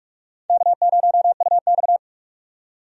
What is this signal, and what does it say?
Morse code, heard as a keyed tone, about 45 wpm, 700 Hz. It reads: K0UX
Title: USA Call Signs - 4 Characters - 45wpm